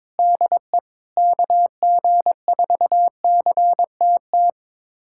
Morse code, heard as a keyed tone, about 22 wpm, 700 Hz. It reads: DE KG4CTT
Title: USA Callsign DE Callsign - 6 Characters - 22wpm